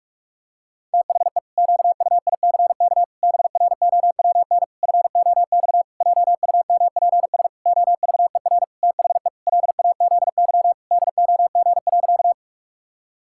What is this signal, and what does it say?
Morse code, heard as a keyed tone, about 48 wpm, 700 Hz. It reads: THE QUICK BROWN FOX JUMPS OVER THE LAZY DOG.
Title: Pangram Challenge v2 - 45wpm